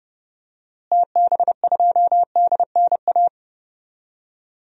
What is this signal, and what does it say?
Morse code, heard as a keyed tone, about 30 wpm, 700 Hz. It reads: TB2DNA
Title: DXCC-100 Call Signs - 6 Characters - Repeated 2x - 30wpm